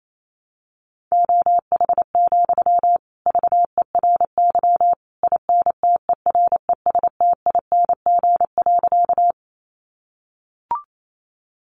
Morse code, heard as a keyed tone, about 28 wpm, 700 Hz. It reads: OH, VERY INTERESTING.
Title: Sentences from Top 1000 Words 28wpm